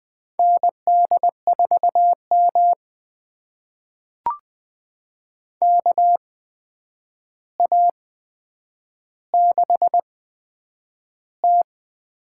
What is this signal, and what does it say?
Morse code, heard as a keyed tone, about 20 wpm, spaced as for 9 wpm, 700 Hz. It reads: ND4M KA6T
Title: USA Call Signs - 8x Character Spacing - 20wpm